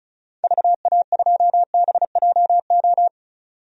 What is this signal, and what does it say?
Morse code, heard as a keyed tone, about 34 wpm, 700 Hz. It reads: VA2BJO
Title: DXCC-100 Call Signs - 6 Characters - 35wpm